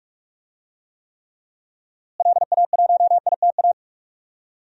Morse code, heard as a keyed tone, about 46 wpm, 700 Hz. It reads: LA1ITA
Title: DXCC-100 Call Signs - 6 Characters - Repeated 3x - 45wpm